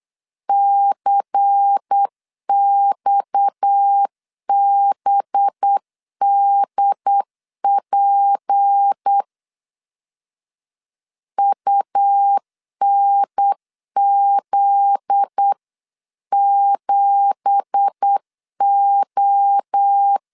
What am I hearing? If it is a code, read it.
Morse code, heard as a keyed tone, about 8 wpm, 800 Hz. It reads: CXBDP UNZ7O
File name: CW / DET / 1 9 1500C S B 8wpm 5624